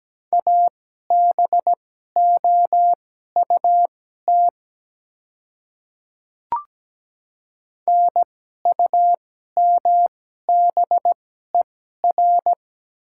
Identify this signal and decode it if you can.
Morse code, heard as a keyed tone, about 17 wpm, spaced as for 12 wpm, 700 Hz. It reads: ABOUT NUMBER
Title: 2x Character Spacing - Top 100 Words - 17wpm